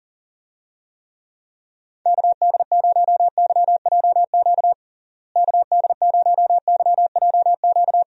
Morse code, heard as a keyed tone, about 40 wpm, 700 Hz. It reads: KD0YJQ KD0YJQ
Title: USA Call Signs - 6 Characters - Repeated 3x - 40wpm